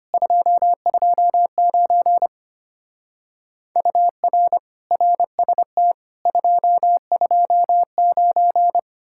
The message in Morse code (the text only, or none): 229 UR RST 229